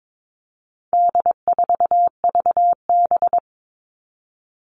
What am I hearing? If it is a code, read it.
Morse code, heard as a keyed tone, about 22 wpm, 700 Hz. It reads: D4VB
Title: All Cut Numbers - 4-Digit 22wpm